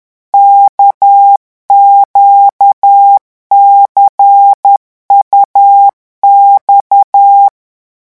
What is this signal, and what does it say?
Morse code, heard as a keyed tone, about 10 wpm, 800 Hz. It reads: KQCUX
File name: morse mp3 / revision2-vitesse 10 mots